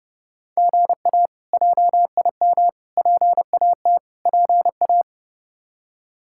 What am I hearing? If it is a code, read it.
Morse code, heard as a keyed tone, about 30 wpm, 700 Hz. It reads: GA JIM PAT PA
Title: K1USN SST QSO Practice - 30wpm